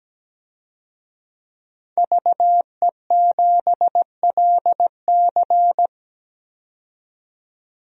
Morse code, heard as a keyed tone, about 17 wpm, 700 Hz. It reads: VE7LC
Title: DXCC-100 Call Signs - 5 Characters - 17wpm